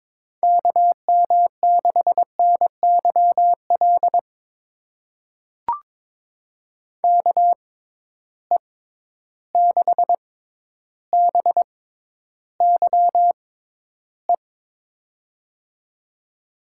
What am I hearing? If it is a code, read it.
Morse code, heard as a keyed tone, about 21 wpm, spaced as for 11 wpm, 700 Hz. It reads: KM6NYL KE6BYE
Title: USA Call Signs - 6 Characters - 6x Character Spacing - 22wpm